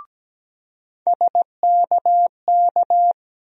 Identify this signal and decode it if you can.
Morse code, heard as a keyed tone, about 17 wpm, 700 Hz. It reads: SKK
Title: USA Call Sign Postfixes - 3 Letters 17wpm